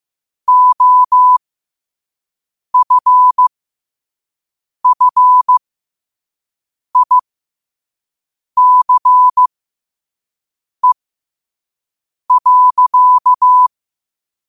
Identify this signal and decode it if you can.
Morse code, heard as a keyed tone, about 15 wpm, spaced as for 5 wpm, 1000 Hz. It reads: OFFICE.
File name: qotdfiles / 5wpm